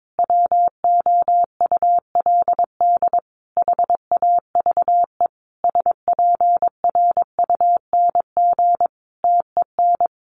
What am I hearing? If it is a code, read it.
Morse code, heard as a keyed tone, about 22 wpm, 700 Hz. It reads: WOULD HAVE SPRUNG TEN
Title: A Sound Of Thunder 22wpm